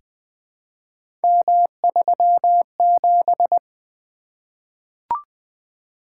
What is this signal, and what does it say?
Morse code, heard as a keyed tone, about 20 wpm, 700 Hz. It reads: M37
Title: Three Letter-Number Combinations - 20wpm